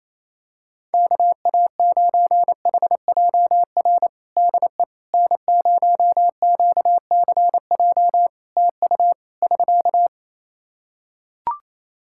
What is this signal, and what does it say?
Morse code, heard as a keyed tone, about 28 wpm, 700 Hz. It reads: KA9HJR DE N0QCJ TU <SK>
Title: Practice QSO - Exchange 4 - Ending - 28wpm